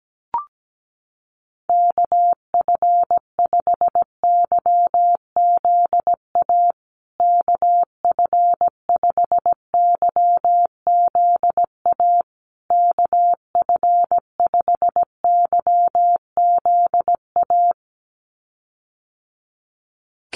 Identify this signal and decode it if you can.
Morse code, heard as a keyed tone, about 17 wpm, 700 Hz. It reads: KF5YZA KF5YZA KF5YZA
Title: USA Call Signs Repeated 3x 17wpm